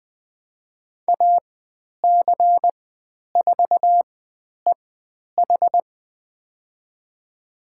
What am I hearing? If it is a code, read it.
Morse code, heard as a keyed tone, about 20 wpm, spaced as for 10 wpm, 700 Hz. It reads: AC4EH
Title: 102 Lesson 17g - Letter u - Call Signs 20wpm